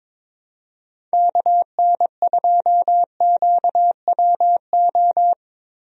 Morse code, heard as a keyed tone, about 22 wpm, 700 Hz. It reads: KN2QWO
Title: USA Call Signs - 6 Characters - Repeated 2x - 22wpm